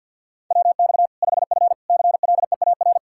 Morse code, heard as a keyed tone, about 46 wpm, 700 Hz. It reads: WX 5F CLEAR
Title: Practice QSO - Exchange 2 - Info - 50wpm